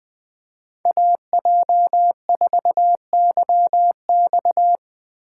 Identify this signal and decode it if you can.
Morse code, heard as a keyed tone, about 20 wpm, 700 Hz. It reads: AJ4YX
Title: Speed Racing - USA 2x2 Call Signs 17wpm